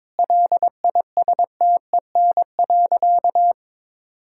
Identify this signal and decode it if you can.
Morse code, heard as a keyed tone, about 22 wpm, 700 Hz. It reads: LISTEN.
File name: practice / 22wpm / Sentences with 4-Word Phrases - 22wpm